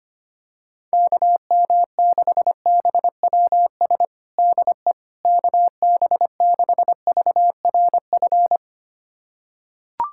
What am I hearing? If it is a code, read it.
Morse code, heard as a keyed tone, about 25 wpm, 700 Hz. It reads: KM6BWS DE KB6VRF
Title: USA Callsign DE Callsign - 6 Characters - 25wpm